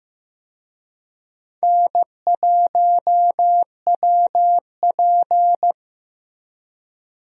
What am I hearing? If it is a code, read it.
Morse code, heard as a keyed tone, about 15 wpm, 700 Hz. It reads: N1WP
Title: USA Call Signs - 4 Characters - Repeated 3x - 15wpm